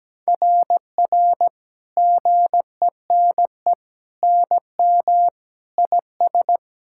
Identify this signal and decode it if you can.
Morse code, heard as a keyed tone, about 17 wpm, 700 Hz. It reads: RR GENE NM IS
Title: SOTA QSO Practice 17wpm